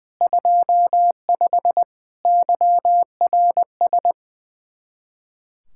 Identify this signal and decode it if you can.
Morse code, heard as a keyed tone, about 20 wpm, 700 Hz. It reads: 25 YRS
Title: Practice QSO - Exchange 3 - More Info - 20wpm